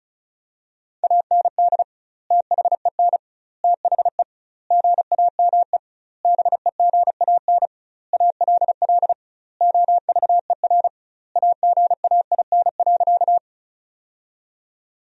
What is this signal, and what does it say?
Morse code, heard as a keyed tone, about 34 wpm, 700 Hz. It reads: AND THEN THE GAME BEGAN ALL OVER AGAIN.
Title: 2x Word-Spacing - Sentences from Top 500 Words - 35wpm